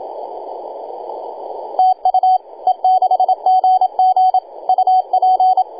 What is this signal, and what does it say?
Morse code, heard as a keyed tone, about 27 wpm, 750 Hz. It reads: TU E6GG UP
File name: e6gg-18-083-cw